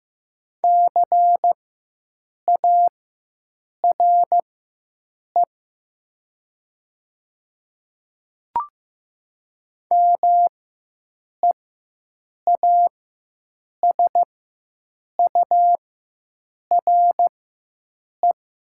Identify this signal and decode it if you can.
Morse code, heard as a keyed tone, about 15 wpm, spaced as for 7 wpm, 700 Hz. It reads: CARE MEASURE
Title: 4x Character Spacing - Top 500 Words - 15wpm